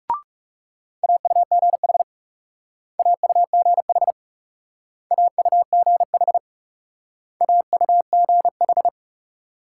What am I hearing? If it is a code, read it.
Morse code, heard as a keyed tone, about 37 wpm, 700 Hz. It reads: AUGH AUGH AUGH AUGH
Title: Speed-Racing Multi-letter Phonemes - 30wpm